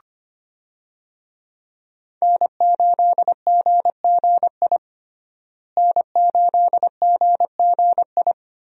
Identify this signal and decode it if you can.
Morse code, heard as a keyed tone, about 25 wpm, 700 Hz. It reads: N8GGI N8GGI
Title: USA Call Signs - 5 Characters - Repeated 2x - 25wpm